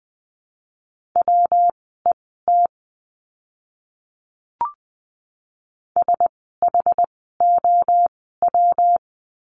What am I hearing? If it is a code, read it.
Morse code, heard as a keyed tone, about 20 wpm, spaced as for 14 wpm, 700 Hz. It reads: WET SHOW
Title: 115 Lesson 19f - Letter w - Words 20wpm